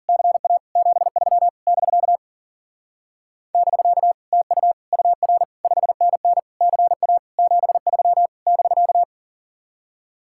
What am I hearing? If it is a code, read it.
Morse code, heard as a keyed tone, about 49 wpm, 700 Hz. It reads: CA 73 <BK> <BK> TU UR 5NN CA 73 <BK>
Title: Speed-Racing Common QSO Phrases Encore - 40wpm